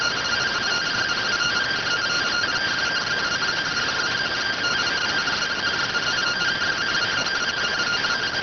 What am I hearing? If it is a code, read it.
Two-tone teleprinter signal, about 170 Hz shift.